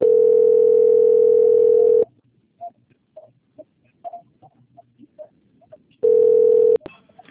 Ringback tone (North America), 440 + 480 Hz.
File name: out- - - - - .35362